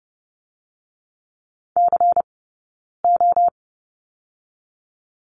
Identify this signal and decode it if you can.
Morse code, heard as a keyed tone, about 30 wpm, 700 Hz. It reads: C O
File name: Lesson 20d - Number 3 - Two Characters 30wpm